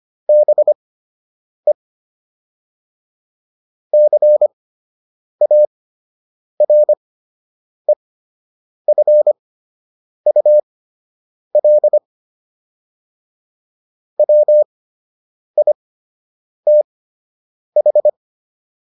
Morse code, heard as a keyed tone, about 25 wpm, spaced as for 8 wpm, 600 Hz. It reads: BE CAREFUL WITH